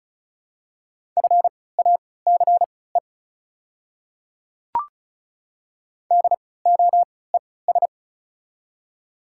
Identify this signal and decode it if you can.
Morse code, heard as a keyed tone, about 34 wpm, spaced as for 20 wpm, 700 Hz. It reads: FACE DOES
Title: 3x Character Spacing - Top 300 Words - 35wpm